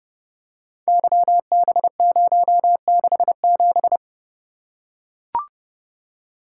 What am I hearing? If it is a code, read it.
Morse code, heard as a keyed tone, about 30 wpm, 700 Hz. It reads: YB067
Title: Code Groups - 5 Letter:Numbers - 30wpm